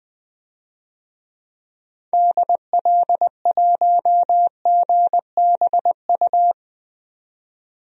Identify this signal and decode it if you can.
Morse code, heard as a keyed tone, about 20 wpm, 700 Hz. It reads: DL1GBU
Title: DXCC-100 Call Signs - 6 Characters - 20wpm